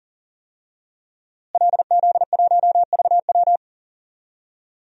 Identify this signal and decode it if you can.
Morse code, heard as a keyed tone, about 40 wpm, 700 Hz. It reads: LZ1VW
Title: DXCC-100 Call Signs - 5 Characters - Repeated 3x - 40wpm